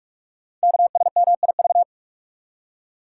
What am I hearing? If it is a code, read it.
Morse code, heard as a keyed tone, about 44 wpm, 700 Hz. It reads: KSMIV
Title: Code Groups - 5 Letters - 45wpm